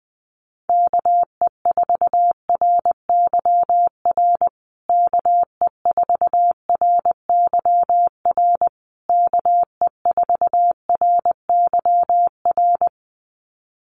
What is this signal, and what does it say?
Morse code, heard as a keyed tone, about 20 wpm, 700 Hz. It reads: KE4RYR KE4RYR KE4RYR
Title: USA Call Signs Repeated 3x 20wpm